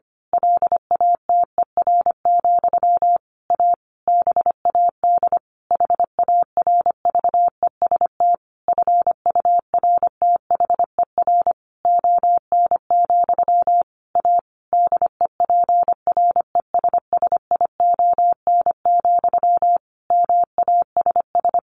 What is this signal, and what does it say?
Morse code, heard as a keyed tone, about 25 wpm, 700 Hz. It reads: LATER, A BAD HARVEST FURTHER ON, A DEPRESSION, MASS
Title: A Sound Of Thunder 25wpm